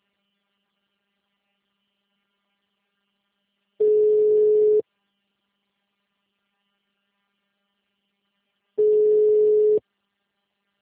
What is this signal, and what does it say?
Ringback tone (Europe, 425 Hz).
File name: out- -2004- - - .51698